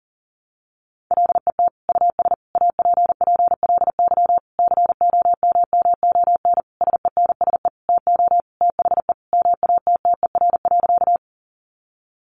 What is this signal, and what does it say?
Morse code, heard as a keyed tone, about 37 wpm, 700 Hz. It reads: LET US APPLY COMMON SENSE TO THE MATTER.
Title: Sentences from Top 1200 Words 40wpm